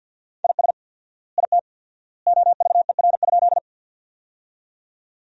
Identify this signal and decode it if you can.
Morse code, heard as a keyed tone, about 50 wpm, 700 Hz. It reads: IS IT OVER?
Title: 4x Word-Spacing - Sentences from Top 100 Words - 50wpm